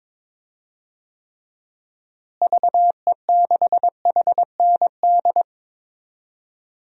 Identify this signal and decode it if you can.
Morse code, heard as a keyed tone, about 22 wpm, 700 Hz. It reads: VE6HND
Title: DXCC-100 Call Signs - 6 Characters - Repeated 2x - 22wpm